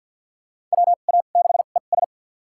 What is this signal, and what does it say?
Morse code, heard as a keyed tone, about 47 wpm, spaced as for 35 wpm, 700 Hz. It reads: WA6ES
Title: USA Call Signs - 5 Characters - 2x Character Spacing - 50wpm